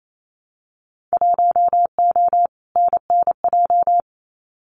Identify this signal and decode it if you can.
Morse code, heard as a keyed tone, about 28 wpm, 700 Hz. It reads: 1O NNJ
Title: Winter Field Day Exchanges 28wpm